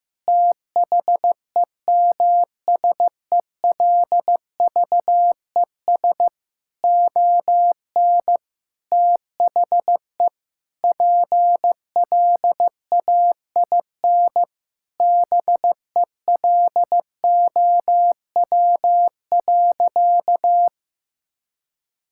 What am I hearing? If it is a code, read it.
Morse code, heard as a keyed tone, about 15 wpm, 700 Hz. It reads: THEMSELVES ON THE PLAIN BELOW.
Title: Sentences from Top 1000 Words - Review Part 5 - 15wpm